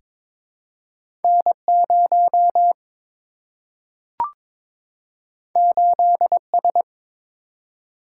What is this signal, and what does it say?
Morse code, heard as a keyed tone, about 22 wpm, 700 Hz. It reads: N 0 8 S
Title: Single-Letter Single-Number 22wpm